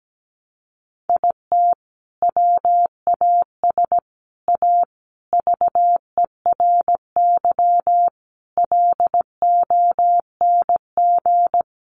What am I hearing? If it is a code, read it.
Morse code, heard as a keyed tone, about 17 wpm, 700 Hz. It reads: IT WAS A VERY LONG
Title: Sentences from Top 1200 Words 17wpm